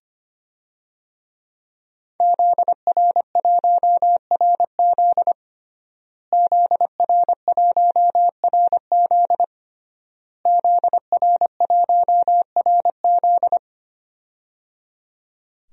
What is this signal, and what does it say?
Morse code, heard as a keyed tone, about 25 wpm, 700 Hz. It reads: ZR1RZ ZR1RZ ZR1RZ
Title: DXCC-100 Call Signs - 5 Characters - Repeated 3x - 25wpm